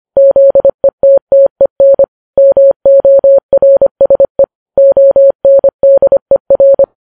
Morse code, heard as a keyed tone, about 25 wpm, 550 Hz. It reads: ZETTEN MORSE ONDER